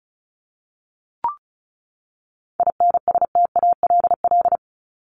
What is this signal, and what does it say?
Morse code, heard as a keyed tone, about 34 wpm, 700 Hz. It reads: INSTALL
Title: Top 1501-1750 Words 35wpm